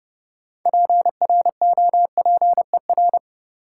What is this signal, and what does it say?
Morse code, heard as a keyed tone, about 32 wpm, 700 Hz. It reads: PROPER
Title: Sentences from Top 800 Words - New Word Drill - 30wpm